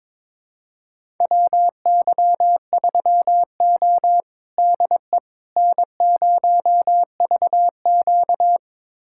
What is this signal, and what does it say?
Morse code, heard as a keyed tone, about 22 wpm, 700 Hz. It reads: WY3O DE N0VQ
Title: USA Callsign DE Callsign - 4 Characters - 22wpm